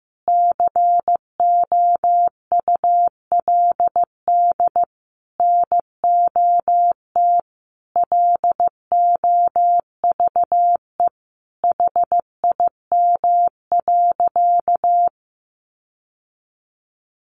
Morse code, heard as a keyed tone, about 15 wpm, 700 Hz. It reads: COULD NOT LOVE HIM.
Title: Sentences from Top 600 Words 15wpm